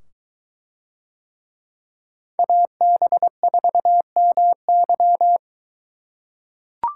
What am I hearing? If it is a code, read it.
Morse code, heard as a keyed tone, about 21 wpm, 700 Hz. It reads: AB4MY